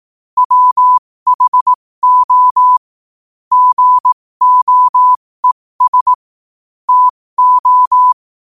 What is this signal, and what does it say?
Morse code, heard as a keyed tone, about 18 wpm, spaced as for 15 wpm, 1000 Hz. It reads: WHO GOES TO
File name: qotdfiles / 15wpm